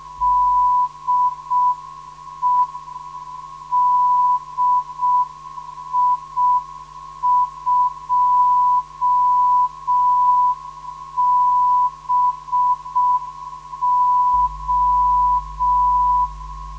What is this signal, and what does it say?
Morse code, heard as a keyed tone, about 6 wpm, 990 Hz. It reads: DE DI2BO
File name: ndb zvok / DI2BO 440 - 2